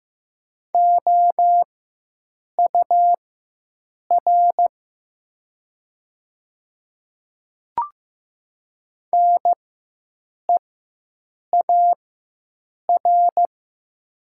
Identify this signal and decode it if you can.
Morse code, heard as a keyed tone, about 15 wpm, spaced as for 7 wpm, 700 Hz. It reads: OUR NEAR
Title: 4x Character Spacing - Top 200 Words - 15wpm